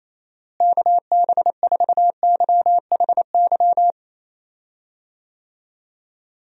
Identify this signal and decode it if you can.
Morse code, heard as a keyed tone, about 28 wpm, 700 Hz. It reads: KB4YHY
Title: USA Call Signs - 6 Characters - Repeated 3x - 28wpm